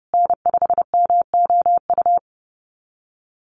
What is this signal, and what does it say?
Morse code, heard as a keyed tone, about 30 wpm, 700 Hz. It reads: N5MOU
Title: USA Call Signs Repeated 3x 30wpm